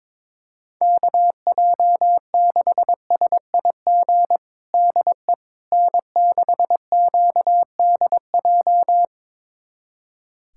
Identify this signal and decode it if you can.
Morse code, heard as a keyed tone, about 22 wpm, 700 Hz. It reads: KJ6SIG DE N6QDJ
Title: USA Callsign DE Callsign 22wpm